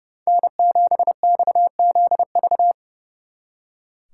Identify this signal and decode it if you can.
Morse code, heard as a keyed tone, about 40 wpm, 700 Hz. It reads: N7XZV